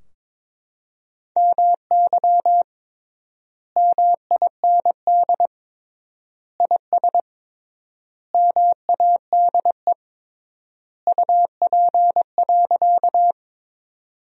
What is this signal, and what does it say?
Morse code, heard as a keyed tone, about 22 wpm, 700 Hz. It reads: MY MIND IS MADE UP.
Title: 3x Word-Spacing - Sentences from Top 500 Words - 22wpm